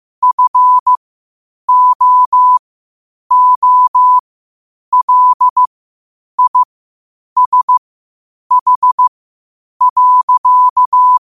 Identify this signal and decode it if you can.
Morse code, heard as a keyed tone, about 15 wpm, spaced as for 8 wpm, 1000 Hz. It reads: FOOLISH.
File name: qotdfiles / 8wpm